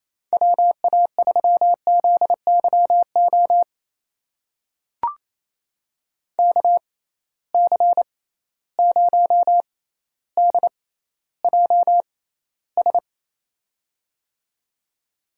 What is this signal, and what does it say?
Morse code, heard as a keyed tone, about 28 wpm, spaced as for 14 wpm, 700 Hz. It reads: WA3ZYO KC0DJS
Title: USA Call Signs - 6 Characters - 6x Character Spacing - 28wpm